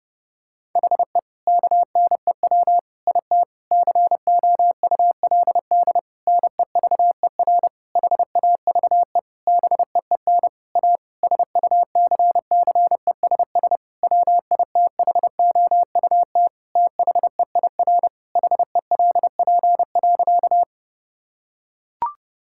Morse code, heard as a keyed tone, about 30 wpm, 700 Hz. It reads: HE KNEW IT COULD NEVER HAVE BEEN A SUCCESS WITHOUT THEIR HELP.
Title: Sentences from Top 1000 Words - Review Part 6 - 30wpm